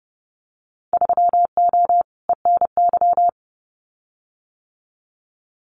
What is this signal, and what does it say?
Morse code, heard as a keyed tone, about 30 wpm, 700 Hz. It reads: 3O ENY
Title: Winter Field Day Exchanges Encore 30wpm